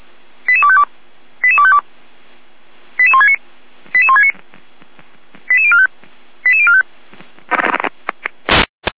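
Five-tone selective call (ZVEI), heard as quick runs of five tones, tones of about 70 ms. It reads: 88252 88252 88169 88169 88045 88045
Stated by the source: ZVEI five-tone selective call received on an SDR.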